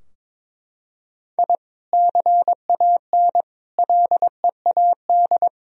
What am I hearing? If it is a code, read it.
Morse code, heard as a keyed tone, about 22 wpm, 700 Hz. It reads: I CAN LEAD
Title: Sentences from Top 1000 Words - Review Part 2 - 22wpm